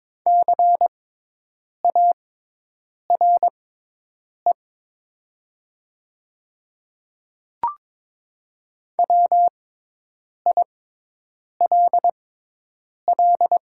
Morse code, heard as a keyed tone, about 22 wpm, spaced as for 8 wpm, 700 Hz. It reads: CARE WILL
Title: 6x Character Spacing - Top 500 Words - 22wpm